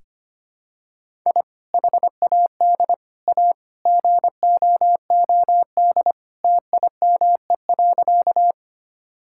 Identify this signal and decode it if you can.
Morse code, heard as a keyed tone, about 25 wpm, 700 Hz. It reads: I HAD A GOOD TIME.
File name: practice / 25wpm / Sentences with 3-Word Phrases - 25wpm